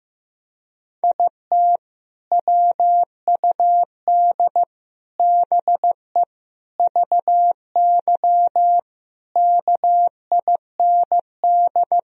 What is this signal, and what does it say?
Morse code, heard as a keyed tone, about 15 wpm, 700 Hz. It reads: IT WUD BE VY KIND
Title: Sentences from Top 200 Words and Abbreviations 15wpm